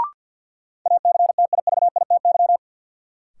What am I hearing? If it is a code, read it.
Morse code, heard as a keyed tone, about 53 wpm, 700 Hz. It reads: ACTIVITY
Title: Sentences from Top 1000 Words - Review Part 4 - New Word Drill - 50wpm